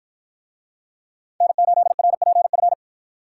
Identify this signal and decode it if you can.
Morse code, heard as a keyed tone, about 49 wpm, 700 Hz. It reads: N8RPF